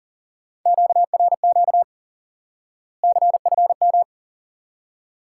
Morse code, heard as a keyed tone, about 40 wpm, 700 Hz. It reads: QRQ CFM
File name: IQE Course 3 - Lesson 89 - YAGI - Two Elements - 40wpm